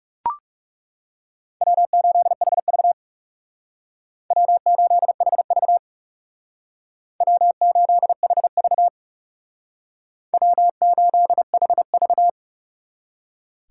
Morse code, heard as a keyed tone, about 37 wpm, 700 Hz. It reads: W8HV W8HV W8HV W8HV
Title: Speed Racing - USA 1x2 Call Signs 30wpm